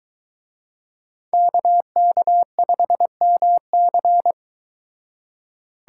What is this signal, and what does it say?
Morse code, heard as a keyed tone, about 23 wpm, 700 Hz. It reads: KK5MC